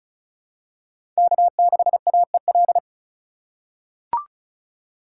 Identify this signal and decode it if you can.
Morse code, heard as a keyed tone, about 34 wpm, 700 Hz. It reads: K6AEL
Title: USA Call Signs - 5 Characters - 4x Character Spacing - 35wpm